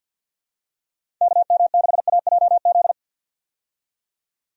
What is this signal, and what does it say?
Morse code, heard as a keyed tone, about 52 wpm, 700 Hz. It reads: KM6AJZ
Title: USA Call Signs - 6 Characters - 50wpm